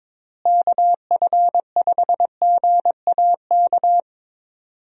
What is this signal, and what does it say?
Morse code, heard as a keyed tone, about 22 wpm, 700 Hz. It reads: KF5GAK
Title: USA Call Signs - 6 Characters - Repeated 3x - 22wpm